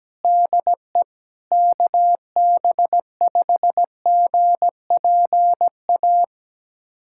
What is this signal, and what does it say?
Morse code, heard as a keyed tone, about 17 wpm, 700 Hz. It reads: DE KB5GPA
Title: USA Callsign DE Callsign - 6 Characters - 17wpm